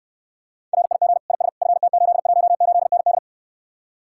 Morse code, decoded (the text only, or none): RIG IS FT818ND